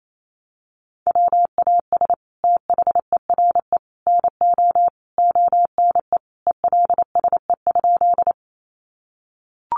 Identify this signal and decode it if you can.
Morse code, heard as a keyed tone, about 28 wpm, 700 Hz. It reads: WAS THERE NO ONE ELSE?
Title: Sentences from Top 800 Words 28wpm